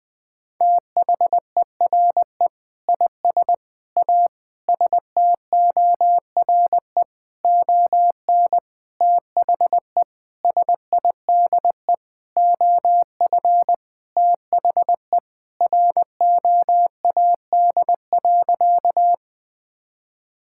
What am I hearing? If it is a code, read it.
Morse code, heard as a keyed tone, about 19 wpm, 700 Hz. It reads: THERE IS A STORE ON THE SIDE OF THE ROAD.
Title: Sentences with 6-Word Phrases - Spoken Beginning - 20wpm